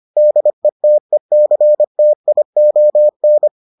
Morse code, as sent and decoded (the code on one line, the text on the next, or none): -.. - . -.-. - .. --- -.
DETECTION